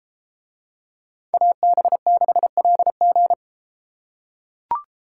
Morse code, heard as a keyed tone, about 32 wpm, 700 Hz. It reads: AB6LG